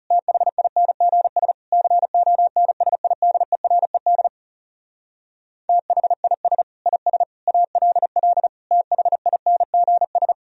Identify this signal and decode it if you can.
Morse code, heard as a keyed tone, about 40 wpm, 700 Hz. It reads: THINGS CONSIDERED THIS IS ALL THINGS